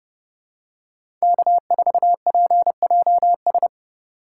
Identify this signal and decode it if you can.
Morse code, heard as a keyed tone, about 29 wpm, 700 Hz. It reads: K4PJS